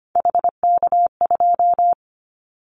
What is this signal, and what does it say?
Morse code, heard as a keyed tone, about 25 wpm, 700 Hz. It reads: HK2
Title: Callsign Prefixes for Top 100 DXCC - 25wpm